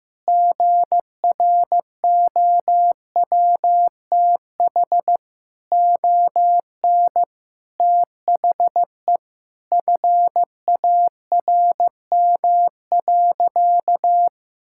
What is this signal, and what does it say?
Morse code, heard as a keyed tone, about 16 wpm, 700 Hz. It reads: GROWTH ON THE FARM.
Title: Sentences from Top 1000 Words - Review Part 5 - 15wpm